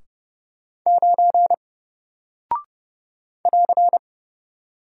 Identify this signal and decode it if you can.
Morse code, heard as a keyed tone, about 29 wpm, 700 Hz. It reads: 9 <AR>
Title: Single Character (Letter:Number:Punctuation:Prosign) 30wpm